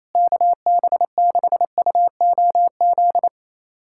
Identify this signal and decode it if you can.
Morse code, heard as a keyed tone, about 28 wpm, 700 Hz. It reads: KB6UOZ
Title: USA Callsign DE Callsign 28wpm